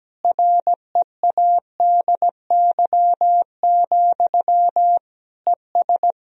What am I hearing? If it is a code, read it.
Morse code, heard as a keyed tone, about 17 wpm, 700 Hz. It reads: READY, ES
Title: Sentences from Top 500 Words and Abbreviations 17wpm